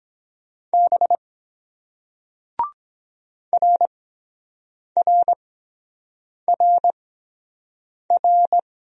Morse code, heard as a keyed tone, about 23 wpm, spaced as for 7 wpm, 700 Hz. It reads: B RRRR